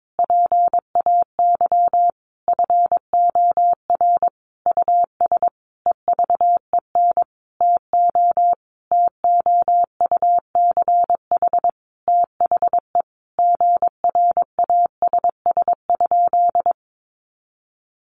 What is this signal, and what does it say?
Morse code, heard as a keyed tone, about 22 wpm, 700 Hz. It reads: PAY FOR US EVEN TO TOUCH THE GRASS?
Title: A Sound Of Thunder 22wpm